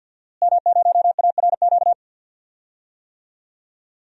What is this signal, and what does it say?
Morse code, heard as a keyed tone, about 51 wpm, 700 Hz. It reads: M0ARQ
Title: DXCC-100 Call Signs - 5 Characters - Repeated 2x - 50wpm